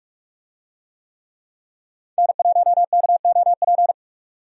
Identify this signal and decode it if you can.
Morse code, heard as a keyed tone, about 44 wpm, 700 Hz. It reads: N1KOP
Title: USA Call Signs - 5 Characters - 45wpm